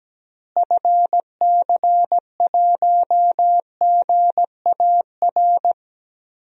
Morse code, heard as a keyed tone, about 17 wpm, 700 Hz. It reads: FC1GAR
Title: DXCC-100 Call Signs - 6 Characters - Repeated 3x - 17wpm